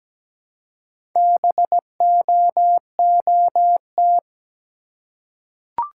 Morse code, heard as a keyed tone, about 17 wpm, 700 Hz. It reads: BOOT
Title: Four Letter Words - 17wpm